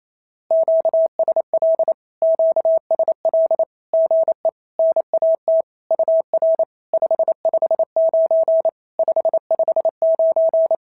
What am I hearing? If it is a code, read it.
Morse code, heard as a keyed tone, about 28 wpm, 650 Hz. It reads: QSL QSL GE NAT UR 559 559
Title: SOTA QSO Practice 28wpm